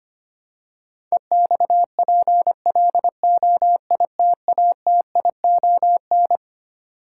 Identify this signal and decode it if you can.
Morse code, heard as a keyed tone, about 25 wpm, 700 Hz. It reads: EXPLOITATION
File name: practice / 25wpm / Twelve Letter Words - 25wpm